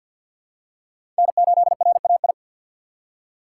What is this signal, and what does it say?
Morse code, heard as a keyed tone, about 52 wpm, 700 Hz. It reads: N8RAI
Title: USA Call Signs - 5 Characters - 3x Character Spacing - 50wpm